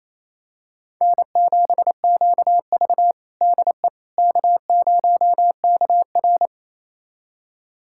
Morse code, heard as a keyed tone, about 28 wpm, 700 Hz. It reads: N7QV DE K0KR
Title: USA Callsign DE Callsign - 4 Characters - 28wpm